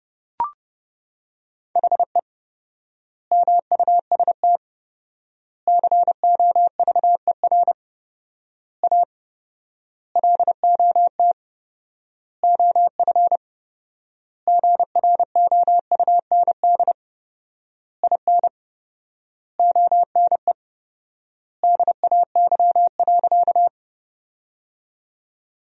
Morse code, heard as a keyed tone, about 30 wpm, 700 Hz. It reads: HE MUST COVER A LOT OF GROUND IN ONE DAY.
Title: 4x Word-Spacing - Sentences from Top 500 Words - 30wpm